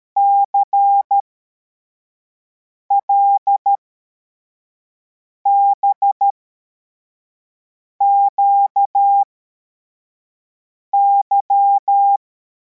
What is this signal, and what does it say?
Morse code, heard as a keyed tone, about 13 wpm, spaced as for 4 wpm, 800 Hz. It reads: CLBQY
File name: formation / cw / Text84